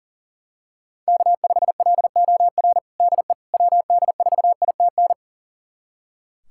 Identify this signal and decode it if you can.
Morse code, heard as a keyed tone, about 40 wpm, 700 Hz. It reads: K5LOR DE WD4ITN
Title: USA Callsign DE Callsign 40wpm